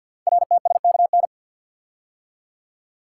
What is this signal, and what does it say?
Morse code, heard as a keyed tone, about 53 wpm, 700 Hz. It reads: RTSKN
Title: Code Groups - 5 Letters - 50wpm